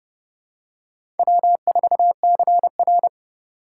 Morse code, heard as a keyed tone, about 31 wpm, 700 Hz. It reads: W4CR
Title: USA Call Signs - 4 Characters - 6x Character Spacing - 30wpm